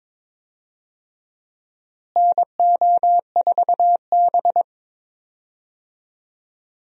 Morse code, heard as a keyed tone, about 22 wpm, 700 Hz. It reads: NO4B
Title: USA Call Signs - 4 Characters - Repeated 2x - 22wpm